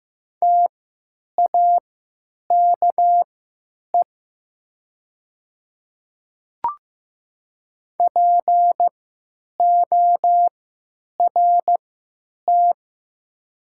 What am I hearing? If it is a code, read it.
Morse code, heard as a keyed tone, about 15 wpm, spaced as for 8 wpm, 700 Hz. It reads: TAKE PORT
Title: 3x Character Spacing - Top 200 Words - 15wpm